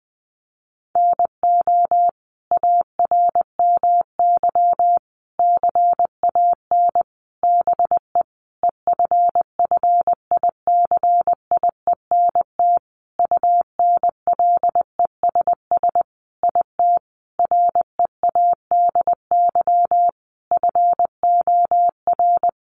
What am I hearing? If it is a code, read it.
Morse code, heard as a keyed tone, about 20 wpm, 700 Hz. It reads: NO ARMY CAN BE EFFICIENT UNLESS IT READY FOR
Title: Sentences from Top 1300 Words 20wpm